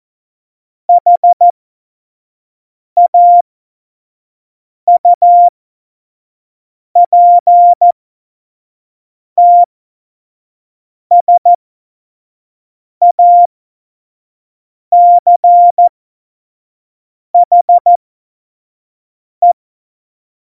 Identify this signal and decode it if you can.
Morse code, heard as a keyed tone, about 14 wpm, spaced as for 5 wpm, 700 Hz. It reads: HAUPTSACHE
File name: Morsekurs / Witze / Witz 14. November 2017, 25 F70